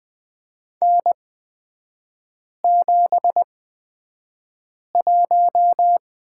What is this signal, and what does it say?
Morse code, heard as a keyed tone, about 20 wpm, spaced as for 5 wpm, 700 Hz. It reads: N71
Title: 243 Lesson 39e - Letter z - Three Characters 20wpm